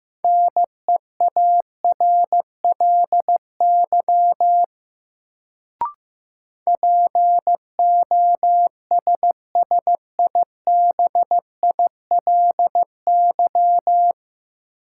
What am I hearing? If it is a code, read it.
Morse code, heard as a keyed tone, about 15 wpm, 700 Hz. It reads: NEARLY POSSIBILY
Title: Sentences from Top 1300 Words - New Word Drill - 15wpm